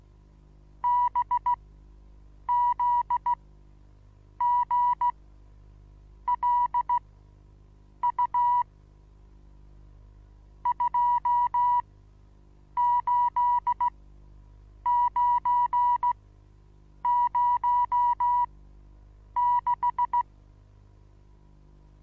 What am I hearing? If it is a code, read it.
Morse code, heard as a keyed tone, about 16 wpm, spaced as for 7 wpm, 970 Hz. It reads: BZGLU 28906
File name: Onlinelehrgang / morsen / DJ4UF Morsen25